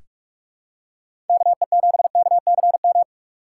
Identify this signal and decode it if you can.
Morse code, heard as a keyed tone, about 46 wpm, 700 Hz. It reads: KE7KCM
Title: USA Call Signs - 6 Characters - 6x Character Spacing - 45wpm